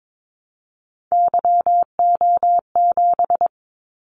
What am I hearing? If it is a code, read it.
Morse code, heard as a keyed tone, about 22 wpm, 700 Hz. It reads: YO7
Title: Callsign Prefixes for Top 25 DXCC - 22wpm